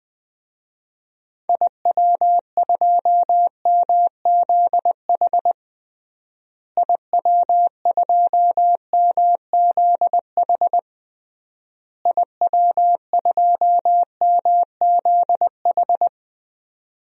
Morse code, heard as a keyed tone, about 20 wpm, 700 Hz. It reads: IW2MZH IW2MZH IW2MZH
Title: DXCC-100 Call Signs - 6 Characters - Repeated 3x - 20wpm